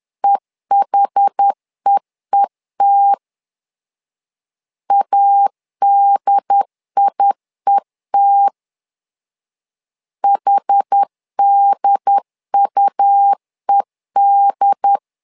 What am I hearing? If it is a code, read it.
Morse code, heard as a keyed tone, about 10 wpm, 800 Hz. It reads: EHEET ADIET HDUED
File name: LEZIONI / 2 UHDN / 2 UHD 10wpm 1826